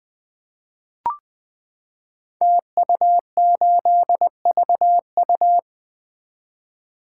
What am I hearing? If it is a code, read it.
Morse code, heard as a keyed tone, about 20 wpm, 700 Hz. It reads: TU8VU
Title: Code Groups - 5 Letter:Numbers - 20wpm